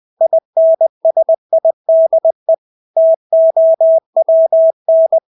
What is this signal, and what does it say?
Morse code, heard as a keyed tone, about 20 wpm, 650 Hz. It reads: INSIDE TOWN
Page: Engelske ord 20 wpm